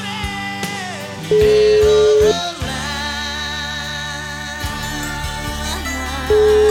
Ringback tone (Europe, 425 Hz), with noise.